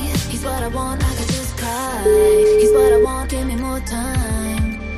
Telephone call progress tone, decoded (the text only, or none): ringback tone